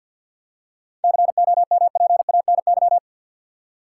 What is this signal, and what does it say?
Morse code, heard as a keyed tone, about 53 wpm, 700 Hz. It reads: COMPANY
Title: Seven Letter Words - 50wpm